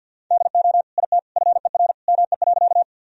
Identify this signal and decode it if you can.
Morse code, heard as a keyed tone, about 50 wpm, 700 Hz. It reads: DO IT FER ME.
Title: Sentences from Top 200 Words and Abbreviations 50wpm